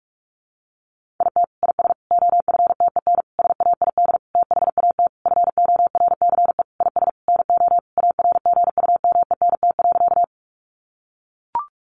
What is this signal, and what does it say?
Morse code, heard as a keyed tone, about 44 wpm, 700 Hz. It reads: IT IS OFTEN SAID THAT FORCE IS NO ARGUMENT.
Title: Sentences from Top 1400 Words 45wpm